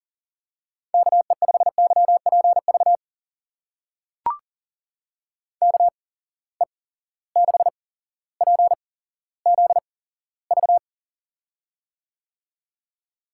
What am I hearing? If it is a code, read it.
Morse code, heard as a keyed tone, about 39 wpm, spaced as for 17 wpm, 700 Hz. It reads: KE5YPV KE6PZV
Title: USA Call Signs - 6 Characters - 8x Character Spacing - 40wpm